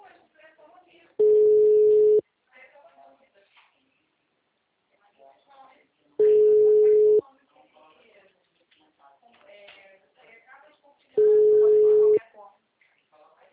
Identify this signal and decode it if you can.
Ringback tone (Europe, 425 Hz), with speech.